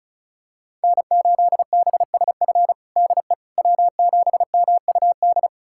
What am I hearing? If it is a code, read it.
Morse code, heard as a keyed tone, about 36 wpm, 700 Hz. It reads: N8BSF DE W7MUD
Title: USA Callsign DE Callsign - 5 Characters - 35wpm